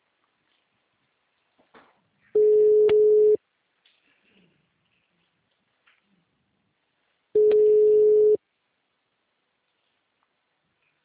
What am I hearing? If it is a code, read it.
Ringback tone (Europe, 425 Hz).